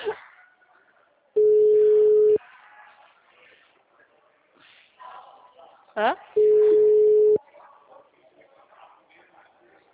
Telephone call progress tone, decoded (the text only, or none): ringback tone